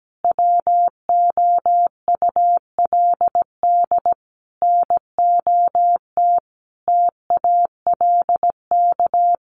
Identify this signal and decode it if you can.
Morse code, heard as a keyed tone, about 17 wpm, 700 Hz. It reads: WOULD NOT TALK
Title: Sentences from Top 800 Words 17wpm